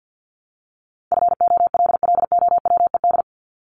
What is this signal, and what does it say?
Morse code, heard as a keyed tone, about 50 wpm, 700 Hz. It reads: FOLLOWED
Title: ED Suffix - Top 100 Words with Base Words 50wpm